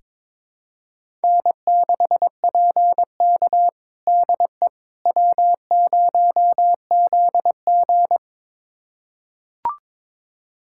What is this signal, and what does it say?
Morse code, heard as a keyed tone, about 22 wpm, 700 Hz. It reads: N6PK DE W0ZG
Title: USA Callsign DE Callsign - 4 Characters - 22wpm